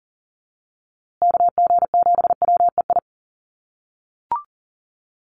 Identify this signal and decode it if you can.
Morse code, heard as a keyed tone, about 39 wpm, 700 Hz. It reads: KG7WEI
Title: USA Call Signs Repeated 2x 40wpm